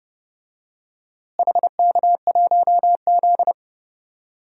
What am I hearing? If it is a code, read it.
Morse code, heard as a keyed tone, about 32 wpm, 700 Hz. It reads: HK1Z
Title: DXCC-100 Call Signs - 4 Characters - Repeated 3x - 30wpm